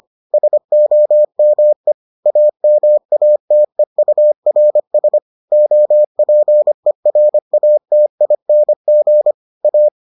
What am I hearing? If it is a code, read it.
Morse code, heard as a keyed tone, about 25 wpm, 600 Hz. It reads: SOME AMATEURS OPERATING A